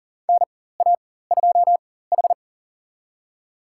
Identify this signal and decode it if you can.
Morse code, heard as a keyed tone, about 39 wpm, spaced as for 18 wpm, 700 Hz. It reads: NA2H
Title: USA Call Signs - 4 Characters - 4x Character Spacing - 40wpm